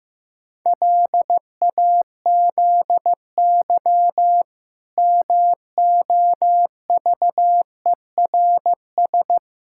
Morse code, heard as a keyed tone, about 16 wpm, 700 Hz. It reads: LAZY MOVERS
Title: Pangram Challenge v2 - 15wpm